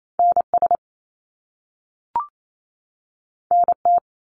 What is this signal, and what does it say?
Morse code, heard as a keyed tone, about 28 wpm, 700 Hz. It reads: NS NT
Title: Canadian Provinces and Territories 28wpm